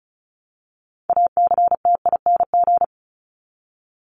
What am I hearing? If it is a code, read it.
Morse code, heard as a keyed tone, about 34 wpm, 700 Hz. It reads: ACTING
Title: ING Suffix - Top 100 Words with Base Words 35wpm